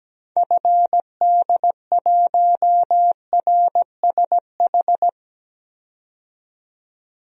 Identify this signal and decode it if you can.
Morse code, heard as a keyed tone, about 17 wpm, 700 Hz. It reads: FD1RSH
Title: DXCC-100 Call Signs - 6 Characters - 17wpm